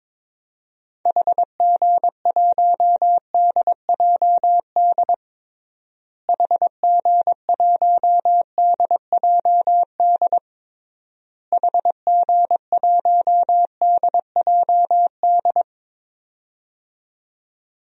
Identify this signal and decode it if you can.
Morse code, heard as a keyed tone, about 22 wpm, 700 Hz. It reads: HG1DJD HG1DJD HG1DJD
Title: DXCC-100 Call Signs - 6 Characters - Repeated 3x - 22wpm